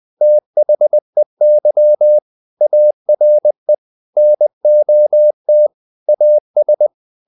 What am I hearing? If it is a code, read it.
Morse code, heard as a keyed tone, about 20 wpm, 600 Hz. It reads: THEY ARE NOT AS
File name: the old man and the sea / the old man and the sea part 1 20wpm